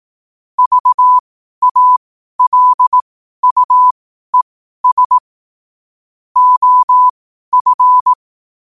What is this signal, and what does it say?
Morse code, heard as a keyed tone, about 18 wpm, spaced as for 12 wpm, 1000 Hz. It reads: VALUES OF
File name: qotdfiles / 12wpm